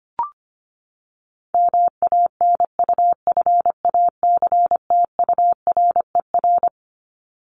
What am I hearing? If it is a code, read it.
Morse code, heard as a keyed tone, about 25 wpm, 700 Hz. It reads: MANUFACTURER
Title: Top 1751-2000 Words 25wpm